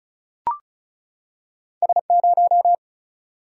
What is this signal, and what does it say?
Morse code, heard as a keyed tone, about 34 wpm, 700 Hz. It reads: S 0
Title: Single-Letter Single-Number 35wpm